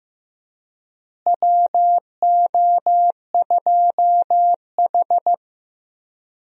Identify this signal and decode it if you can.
Morse code, heard as a keyed tone, about 15 wpm, 700 Hz. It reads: WO2H
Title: USA Call Signs - 4 Characters - Repeated 3x - 15wpm